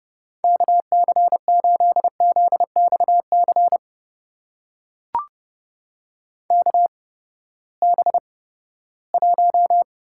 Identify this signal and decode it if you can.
Morse code, heard as a keyed tone, about 30 wpm, spaced as for 17 wpm, 700 Hz. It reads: KC8ZXC KB1
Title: USA Call Signs - 6 Characters - 8x Character Spacing - 30wpm